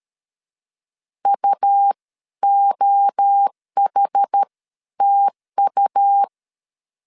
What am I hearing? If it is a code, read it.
Morse code, heard as a keyed tone, about 13 wpm, spaced as for 11 wpm, 800 Hz. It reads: UOHTU
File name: CW / DET / 1 3 1500C S B 12wpm 2828